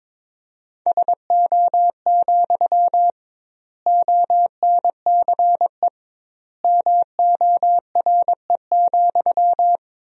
Morse code, heard as a keyed tone, about 22 wpm, 700 Hz. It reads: SO, ONCE MORE,
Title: 2x Word-Spacing - Sentences from Top 300 Words - 22wpm